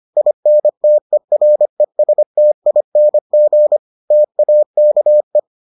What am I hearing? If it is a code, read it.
Morse code, heard as a keyed tone, about 25 wpm, 600 Hz. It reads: INTERESTING TAKE